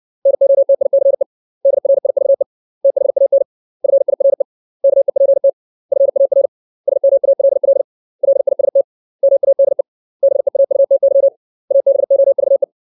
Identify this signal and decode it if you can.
Morse code, heard as a keyed tone, about 60 wpm, 550 Hz. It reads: NOTICE DRIVE THAN FIRE KEPT FAR SMALL LIST MADE BEAUTY ABOVE
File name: 632 words / 632 Words 60WPM